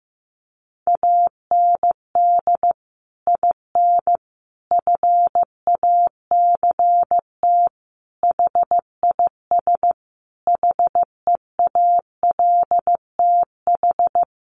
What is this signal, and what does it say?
Morse code, heard as a keyed tone, about 14 wpm, 700 Hz. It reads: AND IN FACT HIS HEALTH
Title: Sentences from Top 1000 Words 15wpm